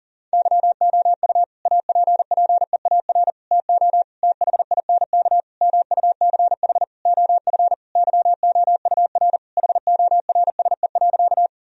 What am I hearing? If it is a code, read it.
Morse code, heard as a keyed tone, about 40 wpm, 700 Hz. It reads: YOU APPEAR TO THINK MUCH OF YOUR HORSE.
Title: Sentences from Top 1000 Words - Review Part 2 - 40wpm